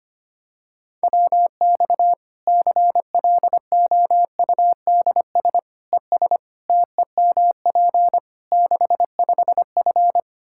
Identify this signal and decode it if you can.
Morse code, heard as a keyed tone, about 25 wpm, 700 Hz. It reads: WX CLOUDS ES TEMP 65F
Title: Practice QSO - Exchange 2 - Info - 25wpm